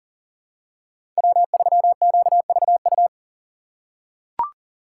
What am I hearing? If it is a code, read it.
Morse code, heard as a keyed tone, about 40 wpm, 700 Hz. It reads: W3QVU
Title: USA Call Signs - 5 Characters - 2x Character Spacing - 40wpm